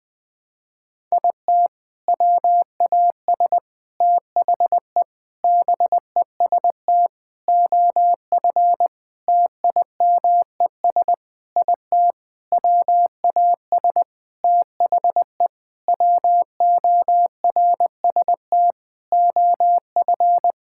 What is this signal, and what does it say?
Morse code, heard as a keyed tone, about 21 wpm, 700 Hz. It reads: IT WAS THE BEST OF TIMES IT WAS THE WORST OF
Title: Sentences with 8-Word Phrases - Spoken Beginning - 20wpm